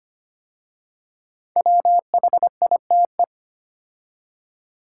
Morse code, heard as a keyed tone, about 25 wpm, 700 Hz. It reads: WHITE
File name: 5 Letter Words - First 3 Letters Given - 25wpm